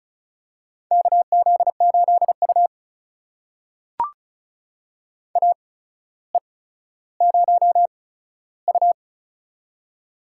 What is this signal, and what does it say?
Morse code, heard as a keyed tone, about 34 wpm, spaced as for 15 wpm, 700 Hz. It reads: KZ8U AE0U
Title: USA Call Signs - 4 Characters - 8x Character Spacing - 35wpm